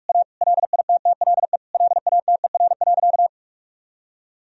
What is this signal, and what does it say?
Morse code, heard as a keyed tone, about 42 wpm, 700 Hz. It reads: A LITTLE LATER.
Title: Sentences from Top 1000 Words - Review Part 3 - 45wpm